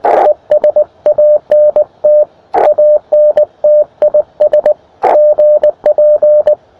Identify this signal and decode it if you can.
Morse code, heard as a keyed tone, about 18 wpm, 600 Hz, with radio noise. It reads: ES ANT ANT IS GP
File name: attachments / ua0sbq 20m